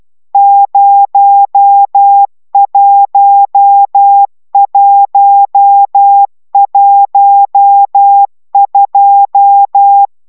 Morse code, heard as a keyed tone, about 12 wpm, 800 Hz. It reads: 01112